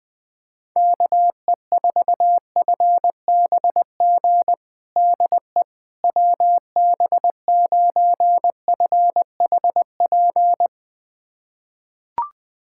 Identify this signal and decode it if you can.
Morse code, heard as a keyed tone, about 20 wpm, 700 Hz. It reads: KE4FBG DE WB9FHP
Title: USA Callsign DE Callsign - 6 Characters - 20wpm